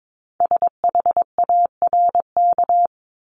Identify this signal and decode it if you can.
Morse code, heard as a keyed tone, about 22 wpm, 700 Hz. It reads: SHARK
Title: Multi-letter Phonemes Repeated 3x 22wpm